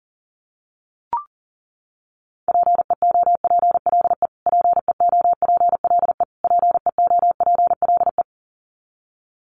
Morse code, heard as a keyed tone, about 40 wpm, 700 Hz. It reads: PEOPLE PEOPLE PEOPLE
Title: Top 500 Words Repeated 40wpm